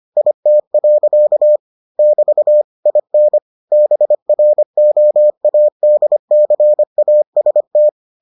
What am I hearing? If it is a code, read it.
Morse code, heard as a keyed tone, about 25 wpm, 600 Hz. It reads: IT. <BT> IN BROADCAST